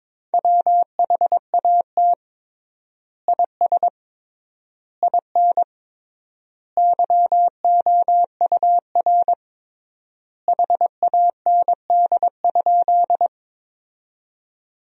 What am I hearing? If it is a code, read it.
Morse code, heard as a keyed tone, about 23 wpm, 700 Hz. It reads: WHAT IS IN YOUR HAND?
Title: 3x Word-Spacing - Sentences from Top 200 Words - 22wpm